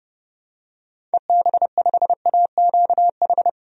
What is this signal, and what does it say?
Morse code, heard as a keyed tone, about 30 wpm, 700 Hz. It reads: EB5AQH
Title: DXCC-100 Call Signs - 6 Characters - Repeated 3x - 30wpm